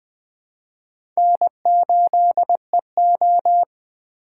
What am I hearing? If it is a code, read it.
Morse code, heard as a keyed tone, about 20 wpm, 700 Hz. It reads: N8EO
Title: USA Call Signs - 4 Characters - 2x Character Spacing - 20wpm